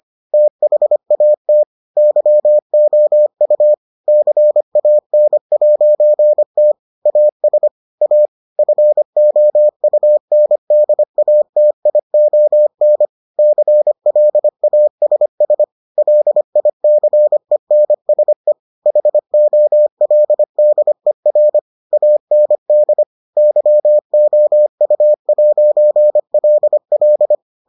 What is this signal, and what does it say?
Morse code, heard as a keyed tone, about 25 wpm, 600 Hz. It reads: THAT YOU CAN'T AS A FOUNDATION CLASS LICENSE HOLDER AND YOU'LL